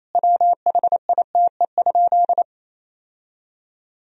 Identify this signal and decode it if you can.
Morse code, heard as a keyed tone, about 28 wpm, 700 Hz. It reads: WHITE?
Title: Sentences from Top 1000 Words - Review Part 1 - 28wpm